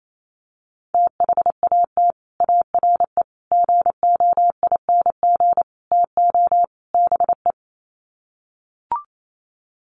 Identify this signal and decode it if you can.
Morse code, heard as a keyed tone, about 28 wpm, 700 Hz. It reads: THAT ARE GOING TO BE
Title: Sets of 5 Words Encore2 28wpm